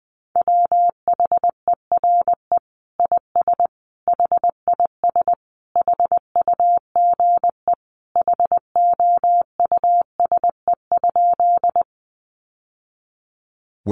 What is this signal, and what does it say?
Morse code, heard as a keyed tone, about 20 wpm, 700 Hz. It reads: WHERE IS HIS HUGE HOUSE?
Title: Sentences from Top 1100 Words 20wpm